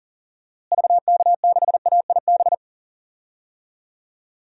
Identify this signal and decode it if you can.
Morse code, heard as a keyed tone, about 40 wpm, 700 Hz. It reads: VK6AIB
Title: DXCC-100 Call Signs - 6 Characters - Repeated 2x - 40wpm